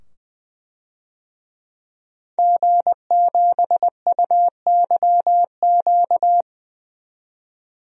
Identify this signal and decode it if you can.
Morse code, heard as a keyed tone, about 20 wpm, 700 Hz. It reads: G7UYQ
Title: DXCC-100 Call Signs - 5 Characters - 20wpm